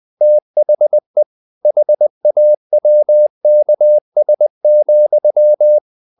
Morse code, heard as a keyed tone, about 20 wpm, 600 Hz. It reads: THE HAWKS,
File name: the old man and the sea / the old man and the sea part 1 20wpm